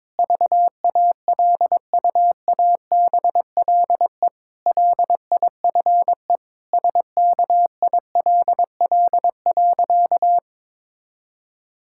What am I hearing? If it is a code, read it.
Morse code, heard as a keyed tone, about 22 wpm, 700 Hz. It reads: VALUABLE LIFE SKILL.
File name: Difficult Sentences with Hints - 22wpm